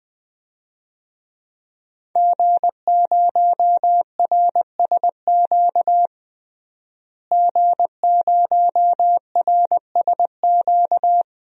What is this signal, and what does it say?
Morse code, heard as a keyed tone, about 21 wpm, 700 Hz. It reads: G0RSQ G0RSQ
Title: DXCC-100 Call Signs - 5 Characters - Repeated 3x - 20wpm